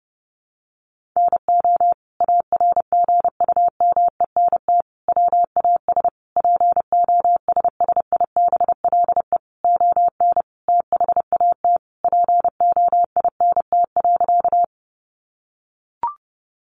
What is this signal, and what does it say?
Morse code, heard as a keyed tone, about 28 wpm, 700 Hz. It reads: NO ARGUMENT WAS POSSIBLE ON THAT POINT.
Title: Sentences from Top 1400 Words 30wpm